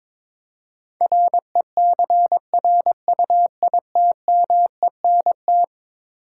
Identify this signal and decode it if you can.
Morse code, heard as a keyed tone, about 22 wpm, 700 Hz. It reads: RECRUITMENT
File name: practice / 22wpm / Eleven Letter Words - 22wpm